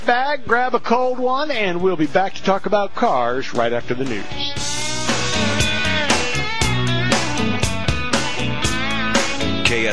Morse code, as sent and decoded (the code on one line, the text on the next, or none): none